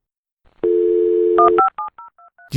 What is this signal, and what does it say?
DTMF, heard as telephone touch tones, tones of about 120 ms. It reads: 157023
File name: dl / gnse-001-modem-dialup-sound